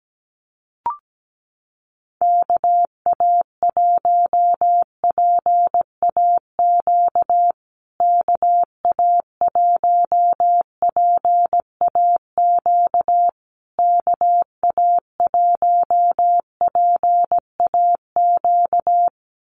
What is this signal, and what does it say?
Morse code, heard as a keyed tone, about 17 wpm, 700 Hz. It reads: KA1PAQ KA1PAQ KA1PAQ
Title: USA Call Signs Repeated 3x 17wpm